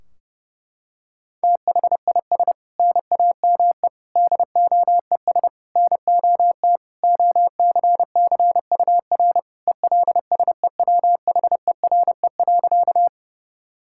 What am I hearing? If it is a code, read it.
Morse code, heard as a keyed tone, about 32 wpm, 700 Hz. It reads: THIS NAME DOES NOT OCCUR ELSEWHERE.
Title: Sentences from Top 1600 Words 30wpm